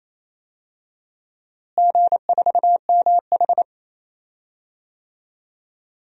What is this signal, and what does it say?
Morse code, heard as a keyed tone, about 27 wpm, 700 Hz. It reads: G4MH
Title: DXCC-100 Call Signs - 4 Characters - Repeated 3x - 28wpm